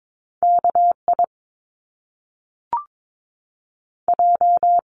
Morse code, heard as a keyed tone, about 22 wpm, 700 Hz. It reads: KI J
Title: Top Two Letter Combinations in Words 22wpm